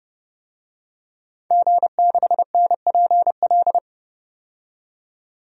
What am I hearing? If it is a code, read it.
Morse code, heard as a keyed tone, about 31 wpm, 700 Hz. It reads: G6NPL
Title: DXCC-100 Call Signs - 5 Characters - 30wpm